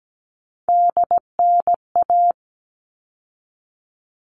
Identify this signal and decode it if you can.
Morse code, heard as a keyed tone, about 17 wpm, 700 Hz. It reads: DNA
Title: All Cut Numbers - 3-Digit 17wpm